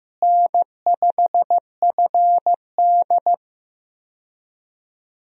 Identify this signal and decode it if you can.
Morse code, heard as a keyed tone, about 16 wpm, 700 Hz. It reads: N5FD
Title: USA Call Signs - 4 Characters - Repeated 2x - 15wpm